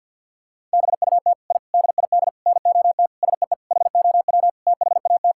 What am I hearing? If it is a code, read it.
Morse code, heard as a keyed tone, about 50 wpm, 700 Hz. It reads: BUT I DID NOT SEE HOW THAT
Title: Sentences from Top 1000 Words - Review Part 5 - 50wpm